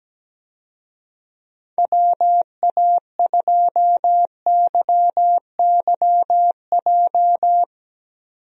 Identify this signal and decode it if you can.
Morse code, heard as a keyed tone, about 17 wpm, 700 Hz. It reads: WA2YYJ
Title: USA Call Signs - 6 Characters - 17wpm